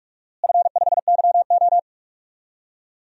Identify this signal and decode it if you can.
Morse code, heard as a keyed tone, about 46 wpm, 700 Hz. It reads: F5YO
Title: DXCC-100 Call Signs - 4 Characters - Repeated 2x - 45wpm